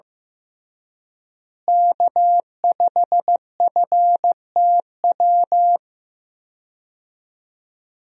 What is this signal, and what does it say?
Morse code, heard as a keyed tone, about 15 wpm, 700 Hz. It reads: K5FTW
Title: USA Call Signs - 5 Characters - Repeated 3x - 15wpm